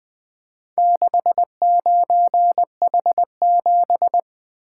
Morse code, heard as a keyed tone, about 20 wpm, 700 Hz. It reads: 69H7
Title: Four Letter-Number Combinations - 20wpm